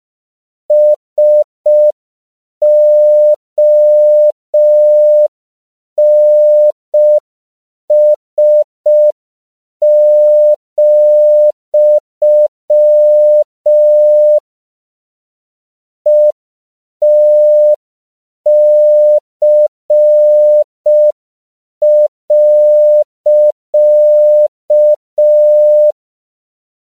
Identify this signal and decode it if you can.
Morse code, heard as a keyed tone, about 5 wpm, 600 Hz. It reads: SONS, ETC.
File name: CW / 5PPM